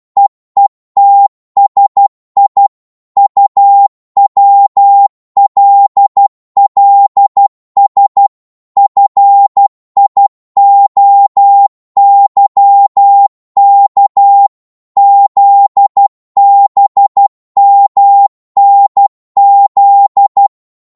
Morse code, heard as a keyed tone, about 12 wpm, 800 Hz. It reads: EETSI UWLLS FIOYK ZBMNZ
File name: Morse / ltr-12wpm